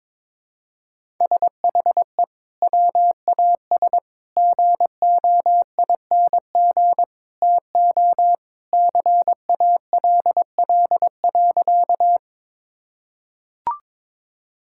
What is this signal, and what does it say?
Morse code, heard as a keyed tone, about 22 wpm, 700 Hz. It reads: SHE WAS GOING TO CALL.
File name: practice / 22wpm / Sentences with 3-Word Phrases - 22wpm